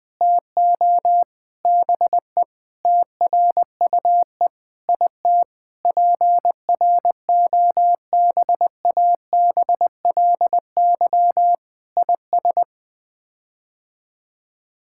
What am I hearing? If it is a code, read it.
Morse code, heard as a keyed tone, about 20 wpm, 700 Hz. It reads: TO BE TRUE IT PROBABLY IS
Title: Sets of 9 Words - 20wpm